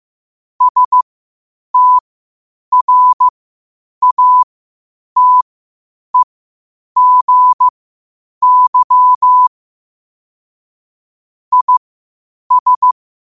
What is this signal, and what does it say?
Morse code, heard as a keyed tone, about 15 wpm, spaced as for 8 wpm, 1000 Hz. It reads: STRATEGY IS
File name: qotdfiles / 8wpm